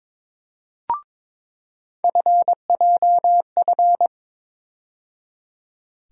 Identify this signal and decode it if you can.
Morse code, heard as a keyed tone, about 22 wpm, 700 Hz. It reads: FJF
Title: USA Call Sign Postfixes - 3 Letters 22wpm